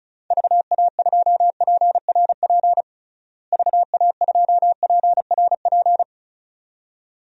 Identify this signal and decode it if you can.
Morse code, heard as a keyed tone, about 34 wpm, 700 Hz. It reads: VA2PRP VA2PRP
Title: DXCC-100 Call Signs - 6 Characters - Repeated 2x - 35wpm